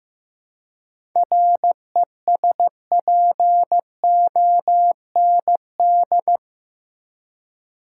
Morse code, heard as a keyed tone, about 15 wpm, 700 Hz. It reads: RESPOND
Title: Seven Letter Words - 15wpm